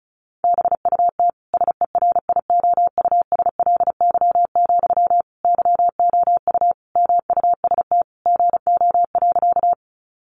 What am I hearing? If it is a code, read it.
Morse code, heard as a keyed tone, about 34 wpm, 700 Hz. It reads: BUT SERIOUSLY, YOU MUST GO.
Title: Sentences from Top 1200 Words 35wpm